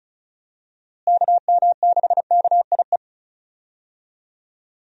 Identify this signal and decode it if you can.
Morse code, heard as a keyed tone, about 34 wpm, 700 Hz. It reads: KM6KIE
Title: USA Call Signs - 6 Characters - Repeated 3x - 35wpm